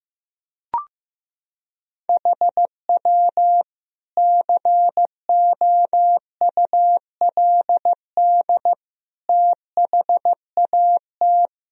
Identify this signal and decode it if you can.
Morse code, heard as a keyed tone, about 15 wpm, 700 Hz. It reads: HW COULD THAT
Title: Sentences from Top 400 Words and Abbreviations 15wpm